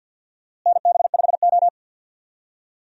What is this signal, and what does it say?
Morse code, heard as a keyed tone, about 51 wpm, 700 Hz. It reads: NB5O
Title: USA Call Signs - 4 Characters - Repeated 2x - 50wpm